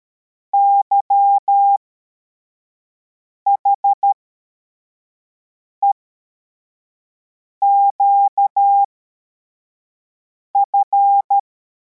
Morse code, heard as a keyed tone, about 13 wpm, spaced as for 4 wpm, 800 Hz. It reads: YHEQF
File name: formation / cw / Text85